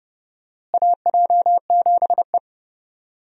Morse code, heard as a keyed tone, about 30 wpm, 700 Hz. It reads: AJ7E
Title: USA Call Signs - 4 Characters - Repeated 2x - 30wpm